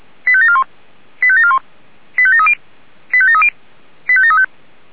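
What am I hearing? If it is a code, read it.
Five-tone selective call (ZVEI), heard as quick runs of five tones, tones of about 70 ms. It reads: 85741 85741 85730 85730 85735
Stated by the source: ZVEI five-tone selective call received on an SDR.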